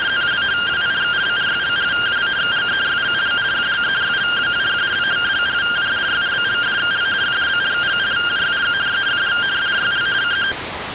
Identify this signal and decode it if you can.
Two-tone teleprinter signal, about 170 Hz shift.
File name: findings / MR AXIAL 0mm 6mm.gen-03 rtty rx